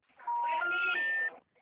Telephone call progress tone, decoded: special information tone

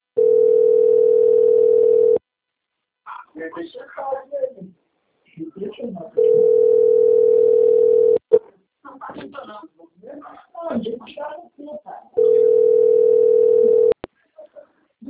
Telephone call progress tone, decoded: ringback tone

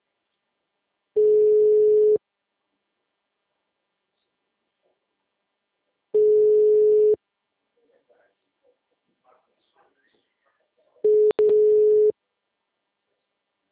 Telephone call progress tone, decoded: ringback tone

